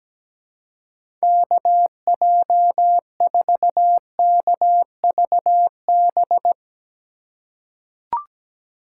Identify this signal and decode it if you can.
Morse code, heard as a keyed tone, about 17 wpm, 700 Hz. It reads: KJ4KVB
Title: USA Call Signs - 3x Character Spacing - 17wpm